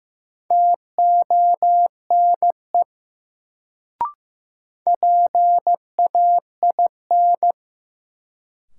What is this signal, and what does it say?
Morse code, heard as a keyed tone, about 15 wpm, 700 Hz. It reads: TONE PAIN
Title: Sentences from Top 1600 Words - New Word Drill - 15wpm